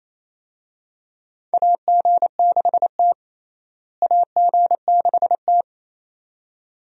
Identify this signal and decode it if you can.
Morse code, heard as a keyed tone, about 28 wpm, 700 Hz. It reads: AG6T AG6T
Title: USA Call Signs - 4 Characters - Repeated 2x - 28wpm